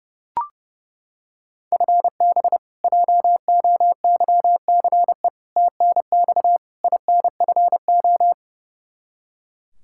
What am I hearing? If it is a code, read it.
Morse code, heard as a keyed tone, about 32 wpm, 700 Hz. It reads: FB JOYCE TNX INFO
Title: K1USN SST QSO Practice 30wpm